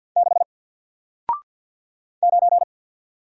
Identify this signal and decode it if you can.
Morse code, heard as a keyed tone, about 48 wpm, 700 Hz. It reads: 6 9
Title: Single Character (Letter:Number:Punctuation:Prosign) Alt 50wpm